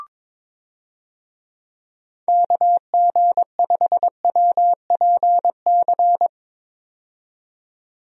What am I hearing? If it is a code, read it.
Morse code, heard as a keyed tone, about 22 wpm, 700 Hz. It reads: KG5WPC
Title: USA Call Signs - 6 Characters - 22wpm